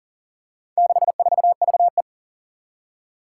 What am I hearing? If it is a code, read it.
Morse code, heard as a keyed tone, about 42 wpm, 700 Hz. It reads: 64VE
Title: Four Challenging Letter-Number Combinations 40wpm